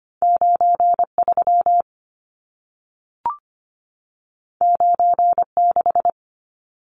Morse code, heard as a keyed tone, about 25 wpm, 700 Hz. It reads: 93 96
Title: Numbers - 2-Digit 25wpm